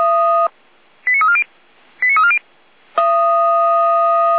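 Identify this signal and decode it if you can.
Five-tone selective call (ZVEI), heard as quick runs of five tones, tones of about 70 ms. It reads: 89340 89340
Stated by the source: ZVEI five-tone selective call received with an SDR.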